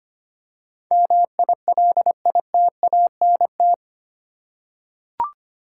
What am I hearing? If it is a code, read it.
Morse code, heard as a keyed tone, about 25 wpm, 700 Hz. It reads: MILITANT
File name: 8 Letter Words - First 3 Letters Given - 25wpm